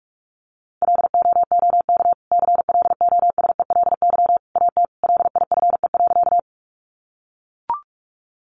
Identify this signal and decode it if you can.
Morse code, heard as a keyed tone, about 44 wpm, 700 Hz. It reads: LOOK CLOSELY AT LIFE.
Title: Sentences from Top 1400 Words 45wpm